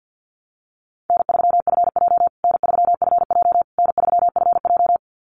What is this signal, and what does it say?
Morse code, heard as a keyed tone, about 50 wpm, 700 Hz. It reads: N3FJ N3FJ N3FJ
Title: USA Call Signs Repeated 3x 50wpm